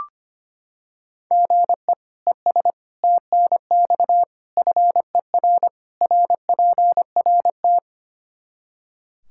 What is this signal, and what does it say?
Morse code, heard as a keyed tone, about 25 wpm, 700 Hz. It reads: GE ES TNX FER RPRT
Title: Practice QSO - Exchange 1 - Beginning - 25wpm